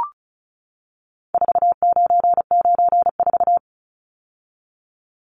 Morse code, heard as a keyed tone, about 34 wpm, 700 Hz. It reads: 4994
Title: Numbers - 4-Digit 35wpm